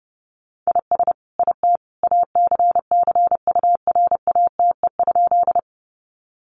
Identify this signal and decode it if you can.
Morse code, heard as a keyed tone, about 30 wpm, 700 Hz. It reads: IS IT ACCURATE?
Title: Sentences from Top 1200 Words 30wpm